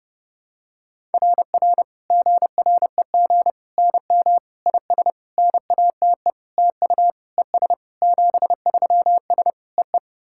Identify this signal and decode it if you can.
Morse code, heard as a keyed tone, about 30 wpm, 700 Hz. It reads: RR GREG NM IS NATE TU ES 73S EE
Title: SOTA QSO Practice 30wpm